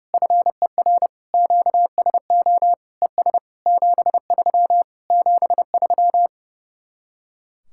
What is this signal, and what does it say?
Morse code, heard as a keyed tone, about 28 wpm, 700 Hz. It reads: FER QSO ES 73 73
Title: Speed-Racing Common QSO Phrases Encore - 30wpm